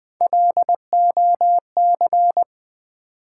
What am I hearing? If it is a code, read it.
Morse code, heard as a keyed tone, about 21 wpm, 700 Hz. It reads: LOC
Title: Build A Word - 3 Letters At A Time - 8-Letter Words - 20wpm